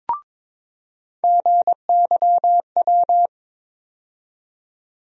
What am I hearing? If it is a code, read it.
Morse code, heard as a keyed tone, about 21 wpm, 700 Hz. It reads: GYW
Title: USA Call Sign Postfixes - 3 Letters 22wpm